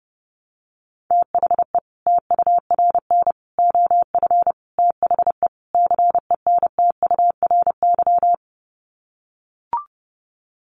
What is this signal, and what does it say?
Morse code, heard as a keyed tone, about 30 wpm, 700 Hz. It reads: THE TURN OF THE CENTURY
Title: Sets of 5 Words 30wpm